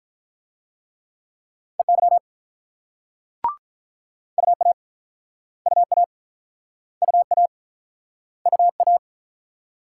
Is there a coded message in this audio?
Yes: Morse code from 1.8 to 9.0 s, heard as a keyed tone, about 47 wpm, 700 Hz. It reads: EY UA UA UA UA